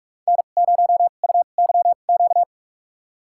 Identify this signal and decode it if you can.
Morse code, heard as a keyed tone, about 44 wpm, spaced as for 32 wpm, 700 Hz. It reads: N0UYQ
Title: USA Call Signs - 2x Character Spacing - 45wpm